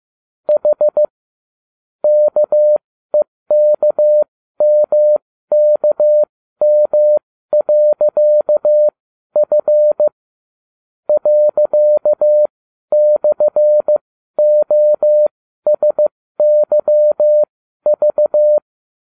Morse code, heard as a keyed tone, about 15 wpm, spaced as for 12 wpm, 600 Hz. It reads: H KEKMKM.F ./OSYV
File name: cw / ubacw29 2